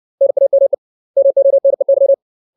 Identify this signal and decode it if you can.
Morse code, heard as a keyed tone, about 60 wpm, 550 Hz. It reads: NAME MONEY
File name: 632 words / 632 Words 60WPM